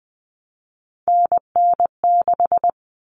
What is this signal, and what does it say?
Morse code, heard as a keyed tone, about 20 wpm, 700 Hz. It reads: NN6
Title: Common Cut Numbers - 3-Digit 20wpm